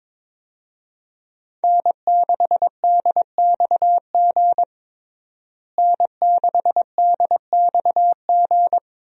Morse code, heard as a keyed tone, about 22 wpm, 700 Hz. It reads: N6DXG N6DXG
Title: USA Call Signs - 5 Characters - Repeated 3x - 22wpm